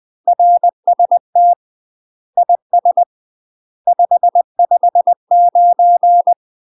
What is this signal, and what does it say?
Morse code, heard as a keyed tone, about 20 wpm, 700 Hz. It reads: RST IS 559
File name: UK MASTER QSOs TRAINING / 20WPM / UK MASTER QSOs 0047